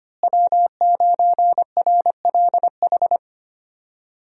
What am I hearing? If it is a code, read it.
Morse code, heard as a keyed tone, about 25 wpm, 700 Hz. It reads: W9RLH
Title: USA Call Signs - 5 Characters - Repeated 2x - 25wpm